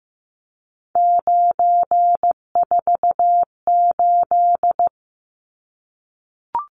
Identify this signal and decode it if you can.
Morse code, heard as a keyed tone, about 15 wpm, 700 Hz. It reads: 948
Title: Numbers - 3-Digit 15wpm